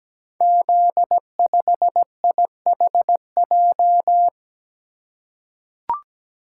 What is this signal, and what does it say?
Morse code, heard as a keyed tone, about 17 wpm, 700 Hz. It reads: Z5IHJ
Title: Code Groups - 5 Letter:Numbers - 17wpm